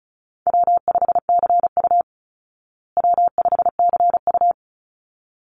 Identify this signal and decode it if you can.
Morse code, heard as a keyed tone, about 34 wpm, 700 Hz. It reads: W5CU W5CU
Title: Call Signs Repeated 3x - 35wpm